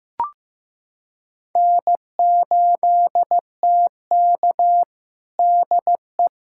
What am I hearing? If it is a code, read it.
Morse code, heard as a keyed tone, about 16 wpm, 700 Hz. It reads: N8TK DE
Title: USA Callsign DE Callsign - 4 Characters - 15wpm